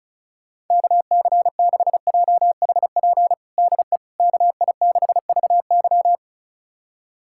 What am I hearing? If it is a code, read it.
Morse code, heard as a keyed tone, about 36 wpm, 690 Hz. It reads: KC6JHP DE KI6VY
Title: USA Callsign DE Callsign 35wpm